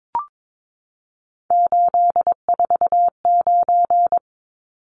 Morse code, heard as a keyed tone, about 22 wpm, 700 Hz. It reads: 849
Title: Numbers - 3-Digit 22wpm